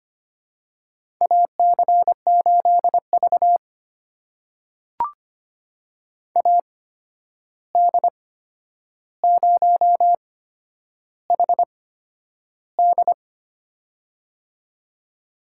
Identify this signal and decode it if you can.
Morse code, heard as a keyed tone, about 25 wpm, spaced as for 10 wpm, 700 Hz. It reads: AC8V AD0HD
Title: USA Call Signs - 8x Character Spacing - 25wpm